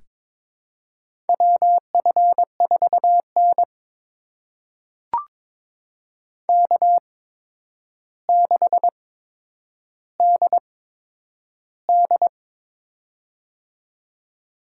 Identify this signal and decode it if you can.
Morse code, heard as a keyed tone, about 22 wpm, spaced as for 9 wpm, 700 Hz. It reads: WF4N K6DD
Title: USA Call Signs - 4 Characters - 8x Character Spacing - 22wpm